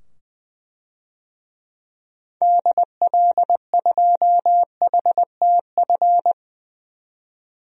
Morse code, heard as a keyed tone, about 20 wpm, 700 Hz. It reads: DL2HTF
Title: DXCC-100 Call Signs - 6 Characters - 20wpm